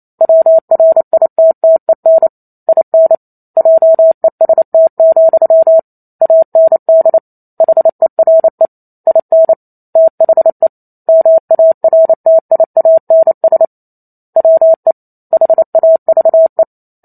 Morse code, heard as a keyed tone, about 28 wpm, 650 Hz. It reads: WRITTEN IN JEST, AND HERE IN THE MARTIANS WE HAVE